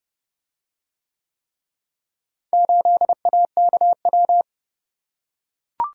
Morse code, heard as a keyed tone, about 28 wpm, 700 Hz. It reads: 8AKW